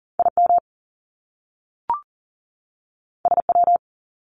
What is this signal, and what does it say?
Morse code, heard as a keyed tone, about 39 wpm, 700 Hz. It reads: IM SW
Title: Top Two Letter Combinations in Words 40wpm